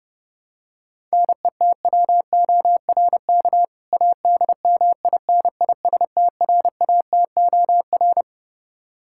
Morse code, heard as a keyed tone, about 31 wpm, 700 Hz. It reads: NETWORK ADMINISTRATOR
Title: K1USN SST QSO Practice 30wpm